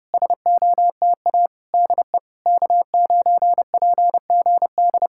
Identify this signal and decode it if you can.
Morse code, heard as a keyed tone, about 30 wpm, 700 Hz. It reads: SOTA DE K9PGD
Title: Speed-Racing Common QSO Phrases - 25wpm